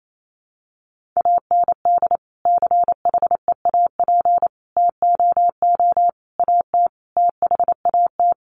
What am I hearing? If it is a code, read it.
Morse code, heard as a keyed tone, about 28 wpm, 700 Hz. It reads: AND CHEAP TOO AT THAT
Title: Sentences from Top 1200 Words 28wpm